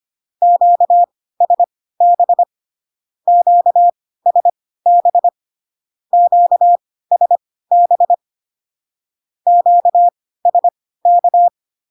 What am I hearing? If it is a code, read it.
Morse code, heard as a keyed tone, about 25 wpm, spaced as for 16 wpm, 700 Hz. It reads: QSB QSB QSB QSK
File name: cw / Q-codes most common repeated 3 times 10 25wpm ik1hge